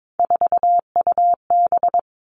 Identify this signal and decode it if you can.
Morse code, heard as a keyed tone, about 22 wpm, 700 Hz. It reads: 4UB
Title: All Cut Numbers - 3-Digit 22wpm